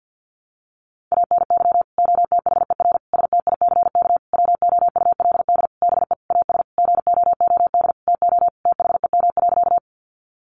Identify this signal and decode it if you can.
Morse code, heard as a keyed tone, about 50 wpm, 700 Hz. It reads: ANY OTHER STICK WOULD BE AS GOOD TO THEM.
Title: Sentences from Top 1200 Words 50wpm